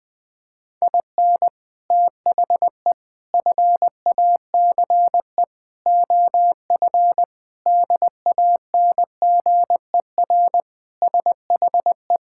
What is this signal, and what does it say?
Morse code, heard as a keyed tone, about 20 wpm, 700 Hz. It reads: IN THE FACE OF DANGER SHE
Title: Sentences with 4-Word Phrases - 20wpm